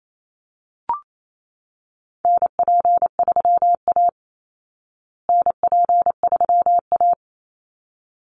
Morse code, heard as a keyed tone, about 28 wpm, 700 Hz. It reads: NP3A NP3A
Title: Call Signs Repeated 3x - 28wpm